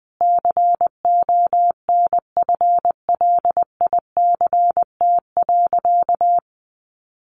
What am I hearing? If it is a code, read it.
Morse code, heard as a keyed tone, about 20 wpm, 700 Hz. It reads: CONFLICT.
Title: Sentences from Top 1300 Words 20wpm